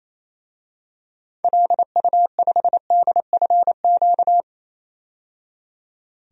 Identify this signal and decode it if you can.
Morse code, heard as a keyed tone, about 28 wpm, 700 Hz. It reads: LU5DFQ
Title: DXCC-100 Call Signs - 6 Characters - 28wpm